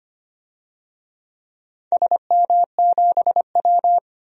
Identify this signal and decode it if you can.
Morse code, heard as a keyed tone, about 25 wpm, 700 Hz. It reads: SM7W
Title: DXCC-100 Call Signs - 4 Characters - 25wpm